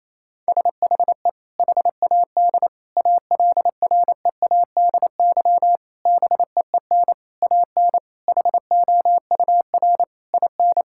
Morse code, heard as a keyed tone, about 28 wpm, 700 Hz. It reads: SHE HAD ALREADY BEEN AN HOUR IN
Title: Sentences from Top 1000 Words - Review Part 6 - 28wpm